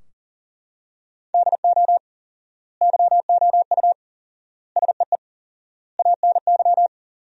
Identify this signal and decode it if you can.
Morse code, heard as a keyed tone, about 40 wpm, 700 Hz. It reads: DO YOU SEE ANY
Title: 4x Word-Spacing - Sentences from Top 500 Words - 40wpm